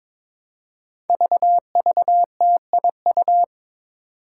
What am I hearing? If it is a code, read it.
Morse code, heard as a keyed tone, about 22 wpm, 700 Hz. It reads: VVTIU
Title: Five Challenging Letter-Number Combinations 22wpm